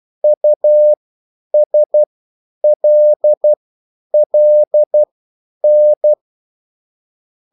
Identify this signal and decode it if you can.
Morse code, heard as a keyed tone, about 12 wpm, spaced as for 9 wpm, 600 Hz. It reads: USLLN